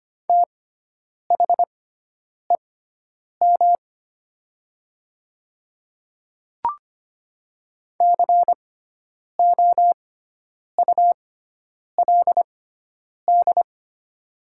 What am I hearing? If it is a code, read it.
Morse code, heard as a keyed tone, about 25 wpm, spaced as for 9 wpm, 700 Hz. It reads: THEM COULD
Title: 6x Character Spacing - Top 100 Words - 25wpm